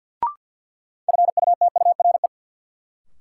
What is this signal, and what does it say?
Morse code, heard as a keyed tone, about 53 wpm, 700 Hz. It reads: FUTURE
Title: Sets of 3 Words - New Word Drill - 50wpm